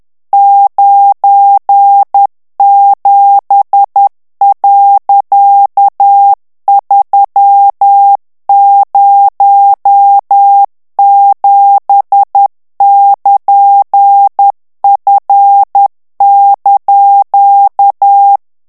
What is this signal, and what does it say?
Morse code, heard as a keyed tone, about 10 wpm, 800 Hz. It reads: 97.307<KN>F)
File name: ANGLAIS texte clair 3 vitesse 10 mots minute